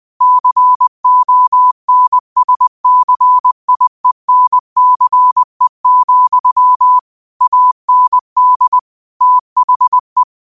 Morse code, heard as a keyed tone, about 20 wpm, 1000 Hz. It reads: CONSCIENCE, AND THE